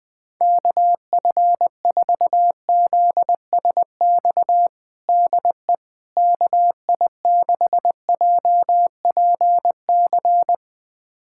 Morse code, heard as a keyed tone, about 20 wpm, 700 Hz. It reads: KF4ZSX DE KI6JPC
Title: USA Callsign DE Callsign 20wpm